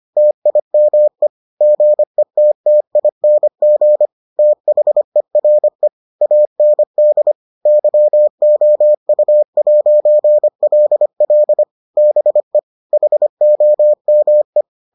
Morse code, heard as a keyed tone, about 25 wpm, 600 Hz. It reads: TIME GETTING THERE AND YOU'LL BE HOME